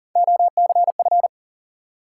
Morse code, heard as a keyed tone, about 43 wpm, 700 Hz. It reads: OCF
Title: Top 180 QSO Elements 40wpm